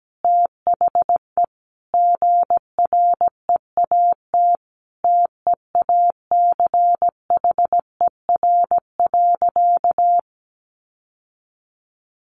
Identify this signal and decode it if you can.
Morse code, heard as a keyed tone, about 17 wpm, 700 Hz. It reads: THE GREAT TEACHER.
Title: Sentences from Top 900 Words 17wpm